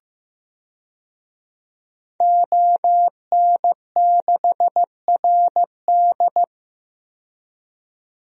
Morse code, heard as a keyed tone, about 15 wpm, 700 Hz. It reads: ON6RD
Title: DXCC-100 Call Signs - 5 Characters - 15wpm